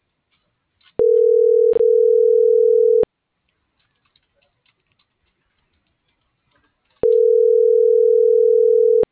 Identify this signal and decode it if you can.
Ringback tone (North America), 440 + 480 Hz.